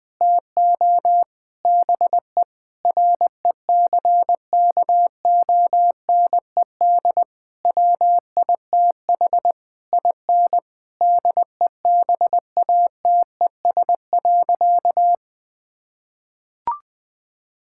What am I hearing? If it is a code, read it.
Morse code, heard as a keyed tone, about 20 wpm, 700 Hz. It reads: TO BE RECKONED WITH IN DEBATES.
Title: Sentences with 6-Word Phrases - Spoken Beginning - 20wpm